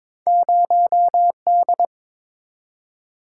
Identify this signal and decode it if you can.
Morse code, heard as a keyed tone, about 22 wpm, 700 Hz. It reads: 0D
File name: practice / 22wpm / Two Letter-Number Combinations - 22wpm